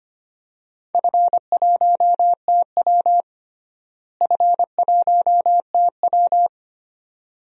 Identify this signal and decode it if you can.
Morse code, heard as a keyed tone, about 25 wpm, 700 Hz. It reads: F1TW F1TW
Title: DXCC-100 Call Signs - 4 Characters - Repeated 2x - 25wpm